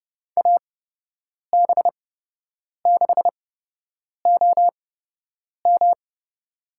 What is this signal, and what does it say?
Morse code, heard as a keyed tone, about 28 wpm, spaced as for 8 wpm, 700 Hz. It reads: AB6OM
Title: USA Call Signs - 8x Character Spacing - 30wpm